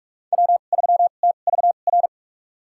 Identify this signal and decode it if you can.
Morse code, heard as a keyed tone, about 46 wpm, spaced as for 32 wpm, 700 Hz. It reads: W3TVR
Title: USA Call Signs - 5 Characters - 2x Character Spacing - 45wpm